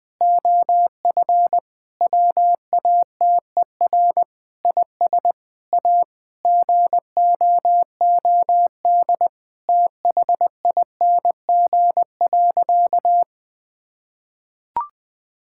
Morse code, heard as a keyed tone, about 21 wpm, 700 Hz. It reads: OF WATER IS A GOOD THING.
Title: Sentences with 3-Word Phrases - 20wpm